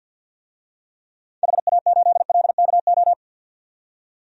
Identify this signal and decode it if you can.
Morse code, heard as a keyed tone, about 47 wpm, 700 Hz. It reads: HA8LKO
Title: DXCC-100 Call Signs - 6 Characters - 50wpm